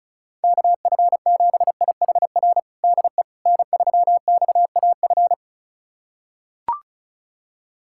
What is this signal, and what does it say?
Morse code, heard as a keyed tone, about 36 wpm, 700 Hz. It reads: KF7IHR DE N3XAF
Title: USA Callsign DE Callsign 35wpm